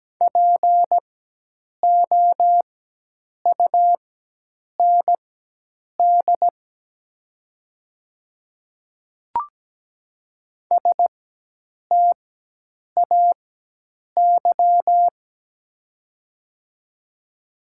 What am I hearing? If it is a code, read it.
Morse code, heard as a keyed tone, about 17 wpm, spaced as for 8 wpm, 700 Hz. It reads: POUND STAY
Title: 4x Character Spacing - Top 500 Words - 17wpm